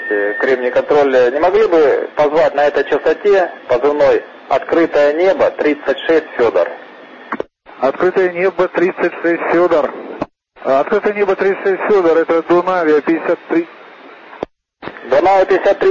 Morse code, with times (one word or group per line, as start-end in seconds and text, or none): none